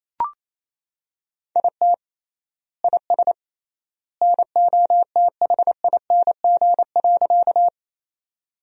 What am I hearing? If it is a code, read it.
Morse code, heard as a keyed tone, about 28 wpm, 700 Hz. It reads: IT IS NOTHING.
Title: 3x Word-Spacing - Sentences from Top 500 Words - 28wpm